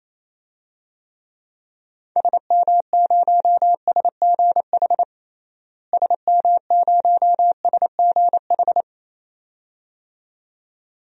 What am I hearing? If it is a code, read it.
Morse code, heard as a keyed tone, about 28 wpm, 700 Hz. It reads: SM0SGH SM0SGH
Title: DXCC-100 Call Signs - 6 Characters - Repeated 2x - 28wpm